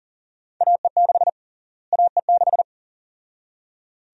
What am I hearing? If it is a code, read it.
Morse code, heard as a keyed tone, about 40 wpm, 700 Hz. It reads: AE6 AE6
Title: USA Call Sign Prefixes - Repeated 2x - 40wpm